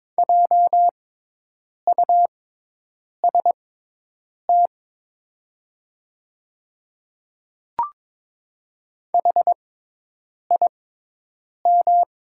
Morse code, heard as a keyed tone, about 21 wpm, spaced as for 8 wpm, 690 Hz. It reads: JUST HIM
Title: 6x Character Spacing - Top 200 Words - 22wpm